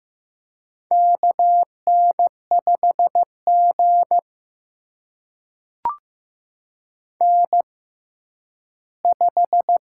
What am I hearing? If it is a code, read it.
Morse code, heard as a keyed tone, about 15 wpm, spaced as for 10 wpm, 700 Hz. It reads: KN5G N5
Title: USA Call Signs - 6x Character Spacing - 15wpm